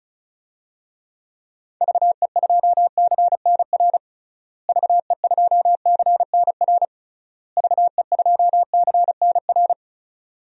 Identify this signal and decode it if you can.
Morse code, heard as a keyed tone, about 34 wpm, 700 Hz. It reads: VE2CNR VE2CNR VE2CNR
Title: DXCC-100 Call Signs - 6 Characters - Repeated 3x - 35wpm